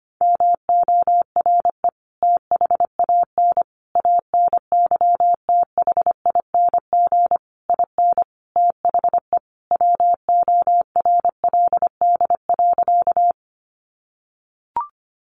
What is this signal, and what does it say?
Morse code, heard as a keyed tone, about 25 wpm, 700 Hz. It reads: MORE THAN ANYTHING IN THE WORLD.
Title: Sentences from Top 1000 Words 25wpm